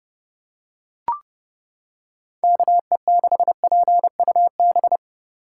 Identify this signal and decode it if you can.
Morse code, heard as a keyed tone, about 29 wpm, 700 Hz. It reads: KE6PUB